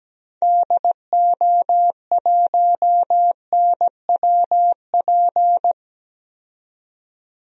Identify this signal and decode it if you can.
Morse code, heard as a keyed tone, about 17 wpm, 700 Hz. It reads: DO1NWP
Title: DXCC-100 Call Signs - 6 Characters - 17wpm